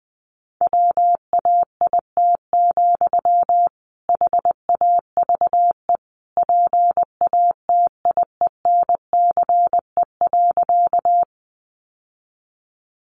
Morse code, heard as a keyed tone, about 19 wpm, 700 Hz. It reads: WAIT, HAVE PATIENCE.
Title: Sentences from Top 1300 Words 20wpm